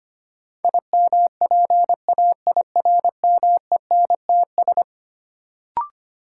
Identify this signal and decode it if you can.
Morse code, heard as a keyed tone, about 25 wpm, 700 Hz. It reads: IMPAIRMENTS
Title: Sets of 5 Words Encore - New Word Drill - 25wpm